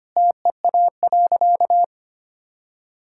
Morse code, heard as a keyed tone, about 25 wpm, 700 Hz. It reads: TEA.
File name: practice / 25wpm / Sentences with 2-Word Phrases - 25wpm